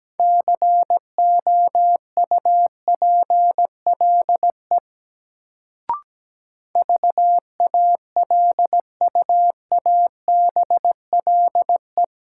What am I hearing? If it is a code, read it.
Morse code, heard as a keyed tone, about 17 wpm, 700 Hz. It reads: COUPLE VALUABLE
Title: Sentences from Top 1100 Words - New Word Drill Alt - 17wpm